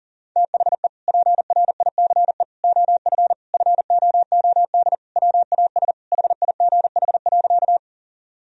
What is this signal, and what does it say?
Morse code, heard as a keyed tone, about 40 wpm, 700 Hz. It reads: THE PRICE OF FOOD WAS HIGH.
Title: Sentences from Top 1000 Words - Review Part 4 - 40wpm